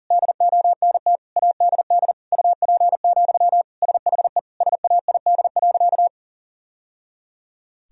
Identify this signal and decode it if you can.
Morse code, heard as a keyed tone, about 40 wpm, 700 Hz. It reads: DONT ADD UP, SHE SAID.
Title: Sentences from Top 1000 Words - Review Part 1 - 40wpm